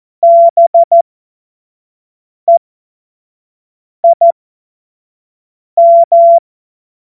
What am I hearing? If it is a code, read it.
Morse code, heard as a keyed tone, about 14 wpm, spaced as for 5 wpm, 680 Hz. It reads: BEIM
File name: Morsekurs / Witze / Witz 18. April 2023 25 F70